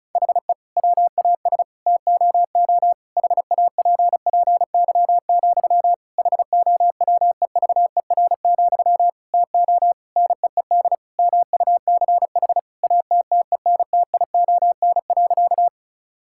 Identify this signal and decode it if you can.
Morse code, heard as a keyed tone, about 34 wpm, 700 Hz. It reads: HE WAS TOO HAPPY, HOWEVER, TO NEED MUCH ATTENTION.
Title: Sentences from Top 1000 Words - Review Part 6 - 35wpm